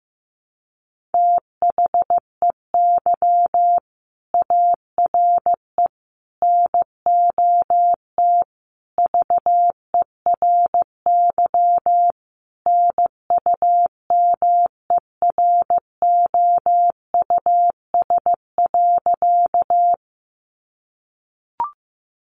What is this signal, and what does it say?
Morse code, heard as a keyed tone, about 15 wpm, 700 Hz. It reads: THEY ARE NOT VERY NUMEROUS.
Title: Sentences from Top 1400 Words 15wpm